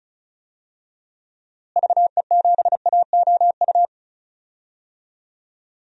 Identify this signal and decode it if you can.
Morse code, heard as a keyed tone, about 34 wpm, 700 Hz. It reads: VE7AOU
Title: DXCC-100 Call Signs - 6 Characters - Repeated 2x - 35wpm